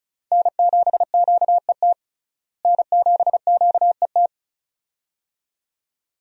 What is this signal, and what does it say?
Morse code, heard as a keyed tone, about 36 wpm, 700 Hz. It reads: N7QET N7QET
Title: USA Call Signs - 5 Characters - Repeated 2x - 35wpm